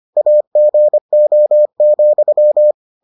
Morse code, heard as a keyed tone, about 25 wpm, 600 Hz. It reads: AGO,